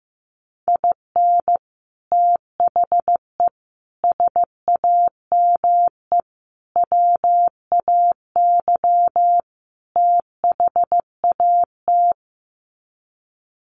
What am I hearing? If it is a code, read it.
Morse code, heard as a keyed tone, about 15 wpm, 700 Hz. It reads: IN THE SAME WAY THAT
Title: Sets of 5 Words 15wpm